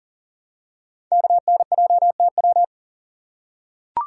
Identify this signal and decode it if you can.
Morse code, heard as a keyed tone, about 42 wpm, 700 Hz. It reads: KNJTW
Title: Code Groups - 5 Letter:Numbers - 40wpm